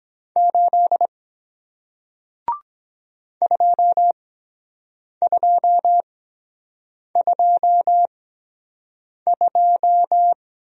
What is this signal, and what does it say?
Morse code, heard as a keyed tone, about 24 wpm, spaced as for 7 wpm, 700 Hz. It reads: 8 2222